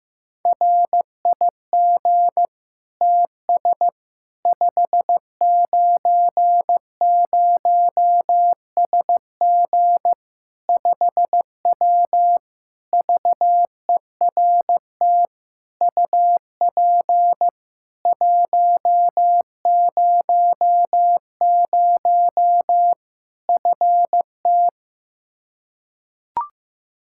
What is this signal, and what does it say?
Morse code, heard as a keyed tone, about 15 wpm, 700 Hz. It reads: RIG TS 590SG 5W VERT UP 100 FT
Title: Practice QSO - Exchange 2 - Info - 15wpm